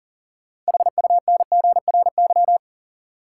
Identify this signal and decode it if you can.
Morse code, heard as a keyed tone, about 40 wpm, 700 Hz. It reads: HUNGRY
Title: Sentences from Top 1400 Words - New Word Drill Alt - 40wpm